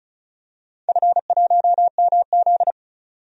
Morse code, heard as a keyed tone, about 36 wpm, 700 Hz. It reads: F1MZ
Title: DXCC-100 Call Signs - 4 Characters - 35wpm